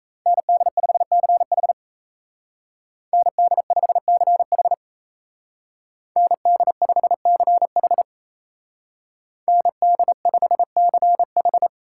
Morse code, heard as a keyed tone, about 33 wpm, 700 Hz. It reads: ND5CH ND5CH ND5CH ND5CH
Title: Speed Racing - USA 2x2 Call Signs 28wpm